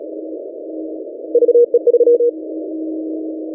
Morse code, heard as a keyed tone, about 35 wpm, 480 Hz. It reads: VE3
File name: posts / ti9-3z9dx-17cw- -2040z